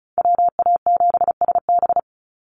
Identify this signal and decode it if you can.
Morse code, heard as a keyed tone, about 34 wpm, 700 Hz. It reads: WA7SB
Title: NA SOTA Chaser Callsigns 35wpm